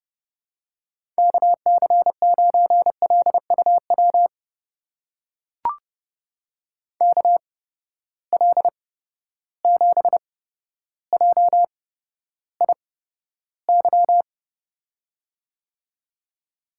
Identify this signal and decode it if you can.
Morse code, heard as a keyed tone, about 31 wpm, spaced as for 13 wpm, 700 Hz. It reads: KC9LUW KL7JIY
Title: USA Call Signs - 6 Characters - 8x Character Spacing - 30wpm